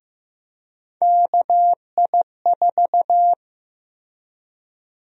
Morse code, heard as a keyed tone, about 15 wpm, 700 Hz. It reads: KI4
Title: USA Call Sign Prefixes - Repeated 2x - 15wpm